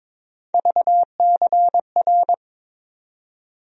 Morse code, heard as a keyed tone, about 22 wpm, 700 Hz. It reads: VCR
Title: USA Call Sign Postfixes - 3 Letters 22wpm